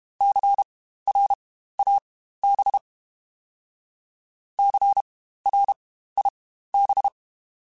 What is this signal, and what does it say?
Morse code, heard as a keyed tone, about 32 wpm, spaced as for 15 wpm, 800 Hz. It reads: CRAB CRIB
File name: Spaced Alphas / Spaced Alphas mp3s / Cs sp @30